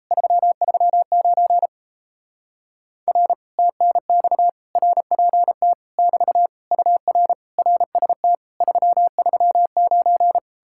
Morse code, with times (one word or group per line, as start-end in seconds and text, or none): 0.1-1.7 339
3.1-3.3 R
3.6-4.5 TNX
4.8-5.7 RPT
6.0-6.5 <BT>
6.7-7.3 UR
7.6-8.4 RST
8.6-10.4 339